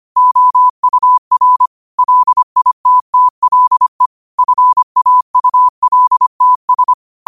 Morse code, heard as a keyed tone, about 25 wpm, 1000 Hz. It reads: OUR LITTLE FAULTS